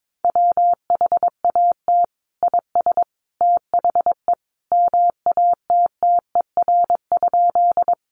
Morse code, heard as a keyed tone, about 22 wpm, 700 Hz. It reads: WHAT IS THE MATTER?
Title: Sentences from Top 700 Words 22wpm